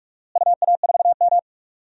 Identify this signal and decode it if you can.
Morse code, heard as a keyed tone, about 46 wpm, 700 Hz. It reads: UA4M
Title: DXCC-100 Call Signs - 4 Characters - Repeated 2x - 45wpm